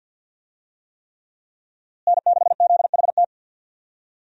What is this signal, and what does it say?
Morse code, heard as a keyed tone, about 51 wpm, 700 Hz. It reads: N6ZHT
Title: USA Call Signs - 5 Characters - Repeated 3x - 50wpm